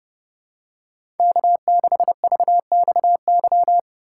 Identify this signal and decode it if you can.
Morse code, heard as a keyed tone, about 30 wpm, 700 Hz. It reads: K6VXY
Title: USA Call Signs - 5 Characters - 4x Character Spacing - 30wpm